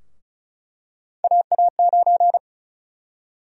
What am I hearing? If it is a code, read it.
Morse code, heard as a keyed tone, about 34 wpm, 700 Hz. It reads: AA9
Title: USA Call Sign Prefixes - Repeated 2x - 35wpm